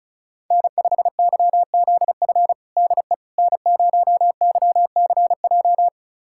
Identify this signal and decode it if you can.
Morse code, heard as a keyed tone, about 36 wpm, 700 Hz. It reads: N5YZF DE N0YCJ
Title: USA Callsign DE Callsign - 5 Characters - 35wpm